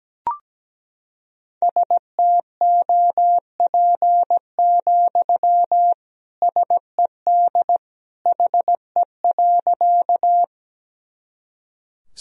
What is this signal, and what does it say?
Morse code, heard as a keyed tone, about 17 wpm, 700 Hz. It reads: STOP, SED HE.
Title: Sentences from Top 300 Words and Abbreviations 17wpm